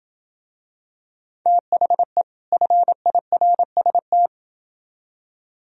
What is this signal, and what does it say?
Morse code, heard as a keyed tone, about 28 wpm, 700 Hz. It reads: THE FIRST